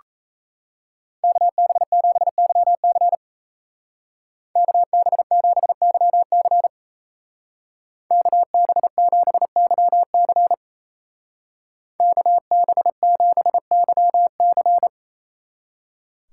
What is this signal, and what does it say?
Morse code, heard as a keyed tone, about 35 wpm, 700 Hz. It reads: KB7YC KB7YC KB7YC KB7YC